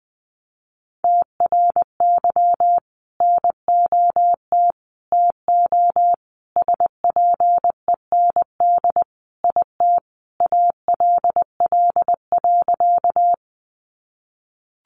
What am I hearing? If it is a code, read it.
Morse code, heard as a keyed tone, about 20 wpm, 700 Hz. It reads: TRY NOT TO SPEND IT ALL.
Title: Sentences from Top 800 Words 20wpm